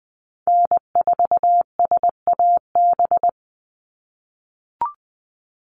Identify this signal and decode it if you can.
Morse code, heard as a keyed tone, about 20 wpm, 700 Hz. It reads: N4SAB
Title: USA Call Signs 20wpm